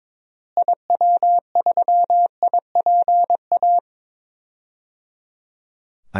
Morse code, heard as a keyed tone, about 22 wpm, 700 Hz. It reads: IW3IPA
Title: DXCC-100 Call Signs - 6 Characters - Repeated 3x - 22wpm